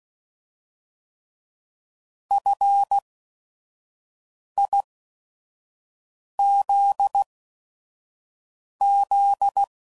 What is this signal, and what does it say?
Morse code, heard as a keyed tone, about 16 wpm, spaced as for 5 wpm, 800 Hz. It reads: FIZZ
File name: Prefixes / By First Letter / Fs